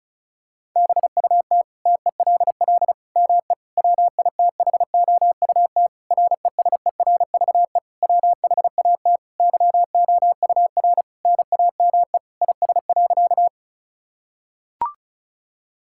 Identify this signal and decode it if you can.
Morse code, heard as a keyed tone, about 36 wpm, 700 Hz. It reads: BUT TELL ME WITHOUT RESERVE WHAT YOUR NAME IS.
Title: Sentences from Top 1000 Words - Review Part 6 - 35wpm